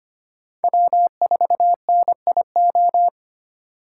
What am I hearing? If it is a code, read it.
Morse code, heard as a keyed tone, about 32 wpm, 700 Hz. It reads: W4NIO